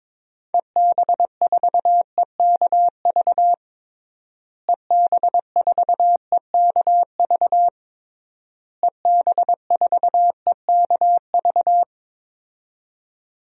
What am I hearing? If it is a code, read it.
Morse code, heard as a keyed tone, about 22 wpm, 700 Hz. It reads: EB4EKV EB4EKV EB4EKV
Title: DXCC-100 Call Signs - 6 Characters - Repeated 3x - 22wpm